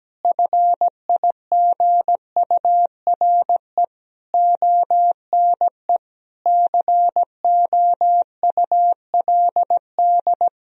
Morse code, heard as a keyed tone, about 17 wpm, 700 Hz. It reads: FIGURE ONE COULD
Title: Sentences from Top 1000 Words - Review Part 2 - 17wpm